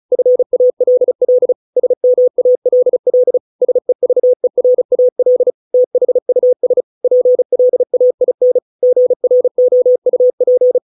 Morse code, heard as a keyed tone, about 34 wpm, 500 Hz. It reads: FALL SMALL SEVERAL THUS PLAIN GROUP